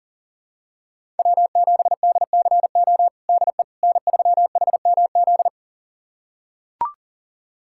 Morse code, heard as a keyed tone, about 37 wpm, 700 Hz. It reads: W7DCO DE N3HMZ
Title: USA Callsign DE Callsign - 5 Characters - 40wpm